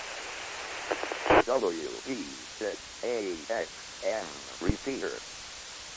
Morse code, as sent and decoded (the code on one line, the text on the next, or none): none